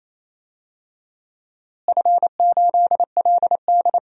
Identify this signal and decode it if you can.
Morse code, heard as a keyed tone, about 27 wpm, 700 Hz. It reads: F8LD
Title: DXCC-100 Call Signs - 4 Characters - Repeated 3x - 28wpm